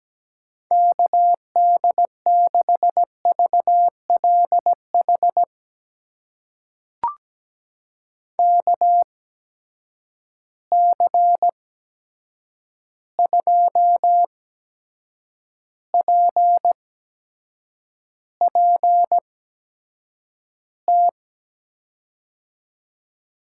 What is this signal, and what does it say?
Morse code, heard as a keyed tone, about 17 wpm, spaced as for 7 wpm, 700 Hz. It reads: KD6VLH KC2PPT
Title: USA Call Signs - 6 Characters - 8x Character Spacing - 17wpm